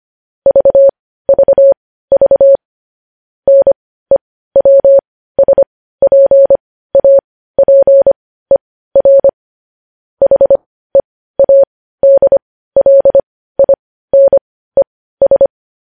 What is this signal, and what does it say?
Morse code, heard as a keyed tone, about 25 wpm, spaced as for 15 wpm, 550 Hz. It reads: VVV NEWSPAPER HEADLINES
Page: news-headlines-in-morse-code-at-15-wpm